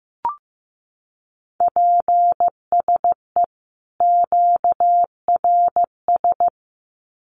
Morse code, heard as a keyed tone, about 15 wpm, 700 Hz. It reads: PSE QRS
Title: Common QSO Phrases 15wpm